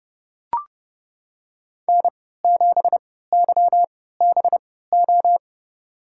Morse code, heard as a keyed tone, about 30 wpm, spaced as for 17 wpm, 700 Hz. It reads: N7YBO
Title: USA Call Signs - 5 Characters - 3x Character Spacing - 30wpm